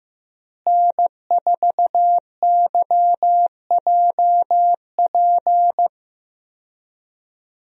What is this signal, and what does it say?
Morse code, heard as a keyed tone, about 15 wpm, 700 Hz. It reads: N4YJP
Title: USA Call Signs - 5 Characters - Repeated 3x - 15wpm